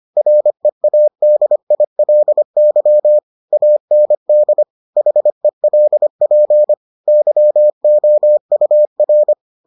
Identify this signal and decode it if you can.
Morse code, heard as a keyed tone, about 25 wpm, 600 Hz. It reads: READILY AND HELP YOUR